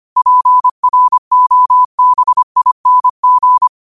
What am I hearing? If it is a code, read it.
Morse code, heard as a keyed tone, about 25 wpm, 1000 Hz. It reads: PROBING
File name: qotdfiles / 25wpm